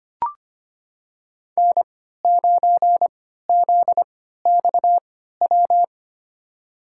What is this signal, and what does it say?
Morse code, heard as a keyed tone, about 25 wpm, spaced as for 14 wpm, 700 Hz. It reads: N9ZXW
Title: USA Call Signs - 5 Characters - 3x Character Spacing - 25wpm